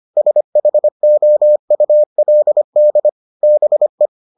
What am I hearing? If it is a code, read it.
Morse code, heard as a keyed tone, about 25 wpm, 600 Hz. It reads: SHOULD BE